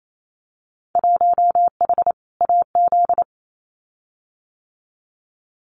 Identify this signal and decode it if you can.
Morse code, heard as a keyed tone, about 28 wpm, 700 Hz. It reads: 1H AZ
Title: Winter Field Day Exchanges 28wpm